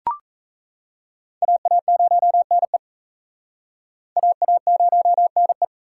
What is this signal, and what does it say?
Morse code, heard as a keyed tone, about 38 wpm, 700 Hz. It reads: AA0NE AA0NE